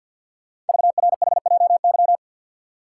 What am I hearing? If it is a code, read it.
Morse code, heard as a keyed tone, about 50 wpm, 700 Hz. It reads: VRHJY
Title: Code Groups - 5 Letter:Numbers - 50wpm